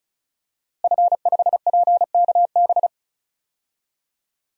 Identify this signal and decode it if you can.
Morse code, heard as a keyed tone, about 36 wpm, 700 Hz. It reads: F5PKB
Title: DXCC-100 Call Signs - 5 Characters - Repeated 3x - 35wpm